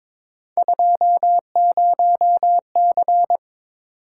Morse code, heard as a keyed tone, about 26 wpm, 700 Hz. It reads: 20C
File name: Speed-Racing - Three Letter-Number Combinations - 22wpm